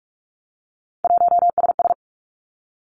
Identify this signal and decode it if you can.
Morse code, heard as a keyed tone, about 46 wpm, 700 Hz. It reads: 1SS
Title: Three Challenging Letter-Number Combinations 45wpm